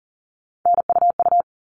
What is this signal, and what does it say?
Morse code, heard as a keyed tone, about 40 wpm, 700 Hz. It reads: NUU
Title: All Cut Numbers - 3-Digit 40wpm